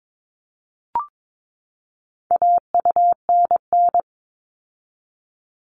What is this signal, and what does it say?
Morse code, heard as a keyed tone, about 22 wpm, 700 Hz. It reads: AUNN
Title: All Cut Numbers - 4-Digit 22wpm